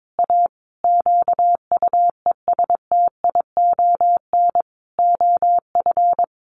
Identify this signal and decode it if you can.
Morse code, heard as a keyed tone, about 22 wpm, 700 Hz. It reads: A QUESTION OF
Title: Sentences from Top 1400 Words 22wpm